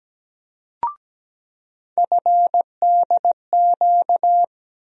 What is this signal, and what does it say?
Morse code, heard as a keyed tone, about 17 wpm, 700 Hz. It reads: FDQ
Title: USA Call Sign Postfixes - 3 Letters 17wpm